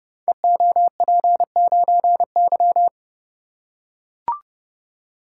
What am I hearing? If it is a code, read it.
Morse code, heard as a keyed tone, about 30 wpm, 700 Hz. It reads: EOP9Y
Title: Code Groups - 5 Letter:Numbers - 30wpm